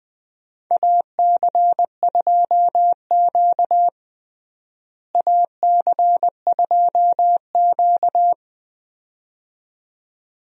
Morse code, heard as a keyed tone, about 20 wpm, 700 Hz. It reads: AC2Q AC2Q
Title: USA Call Signs - 4 Characters - Repeated 2x - 20wpm